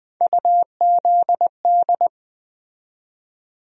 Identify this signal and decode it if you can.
Morse code, heard as a keyed tone, about 20 wpm, 700 Hz. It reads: UZD
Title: USA Call Sign Postfixes - 3 Letters 20wpm